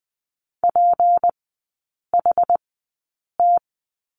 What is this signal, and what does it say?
Morse code, heard as a keyed tone, about 20 wpm, spaced as for 8 wpm, 700 Hz. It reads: PHT
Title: 152 Lesson 25e - Letter p - Three Characters 20wpm